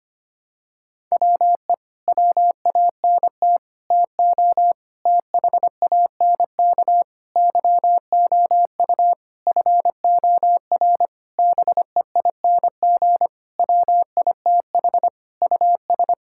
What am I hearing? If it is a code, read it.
Morse code, heard as a keyed tone, about 25 wpm, 700 Hz. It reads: WE WANT TO THANK YOU FOR BEING WITH US
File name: practice / 25wpm / Sentences with 9-Word Phrases - 25wpm